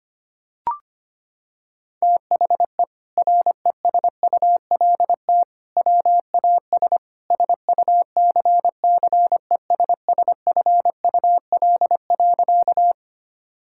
Morse code, heard as a keyed tone, about 25 wpm, 700 Hz. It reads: THE RESULT WAS SUCCESSFUL.
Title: Sentences from Top 1000 Words - Review Part 6 - 25wpm